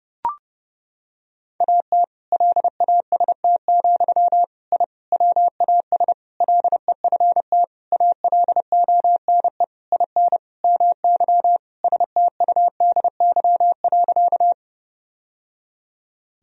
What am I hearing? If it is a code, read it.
Morse code, heard as a keyed tone, about 30 wpm, 700 Hz. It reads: AT LAST, I WAS LEFT ALONE IN MY STUDY.
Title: Sentences from Top 1000 Words - Review Part 1 - 30wpm